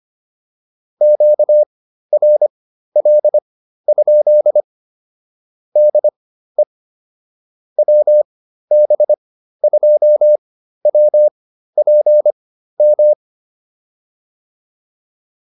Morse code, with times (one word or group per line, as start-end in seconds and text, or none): 1.0-4.6 QRL?
5.8-6.6 DE
7.8-13.1 WB2WPM